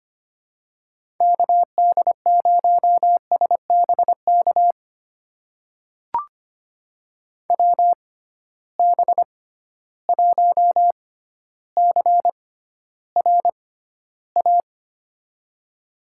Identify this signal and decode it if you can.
Morse code, heard as a keyed tone, about 25 wpm, spaced as for 13 wpm, 700 Hz. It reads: KD0SBK WB1CRA
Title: USA Call Signs - 6 Characters - 6x Character Spacing - 25wpm